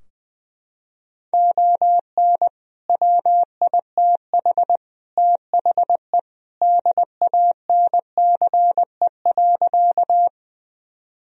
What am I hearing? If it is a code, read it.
Morse code, heard as a keyed tone, about 21 wpm, 700 Hz. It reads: ON WITH THE DANCE.
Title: Sentences from Top 1000 Words - Review Part 4 - 20wpm